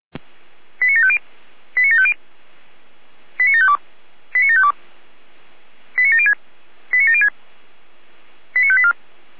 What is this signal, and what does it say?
Five-tone selective call (ZVEI), heard as quick runs of five tones, tones of about 70 ms. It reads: 89750 89750 89752 89752 89706 89706 89584